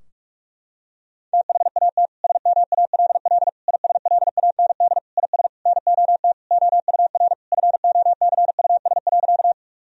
Morse code, heard as a keyed tone, about 46 wpm, 700 Hz. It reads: THAT SMALL ISLAND IS NOT OUR FOCUS.
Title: Sentences from Top 1000 Words - Review Part 4 - 45wpm